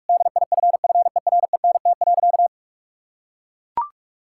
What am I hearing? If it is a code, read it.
Morse code, heard as a keyed tone, about 41 wpm, 700 Hz. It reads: DIFFERENT.
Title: Sentences from Top 1600 Words 45wpm